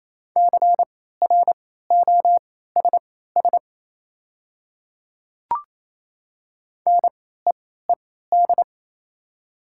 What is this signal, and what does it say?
Morse code, heard as a keyed tone, about 28 wpm, spaced as for 16 wpm, 700 Hz. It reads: CROSS NEED
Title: 3x Character Spacing - Top 300 Words - 28wpm